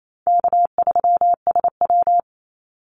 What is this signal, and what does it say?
Morse code, heard as a keyed tone, about 28 wpm, 700 Hz. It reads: K3SW
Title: Call Signs Repeated 3x - 28wpm